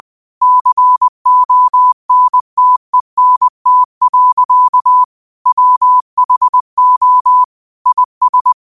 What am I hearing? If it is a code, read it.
Morse code, heard as a keyed tone, about 20 wpm, 1000 Hz. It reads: CONTENT. WHO IS